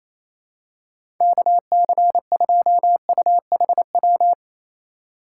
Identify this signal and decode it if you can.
Morse code, heard as a keyed tone, about 28 wpm, 700 Hz. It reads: KC2UHW
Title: USA Call Signs - 2x Character Spacing - 28wpm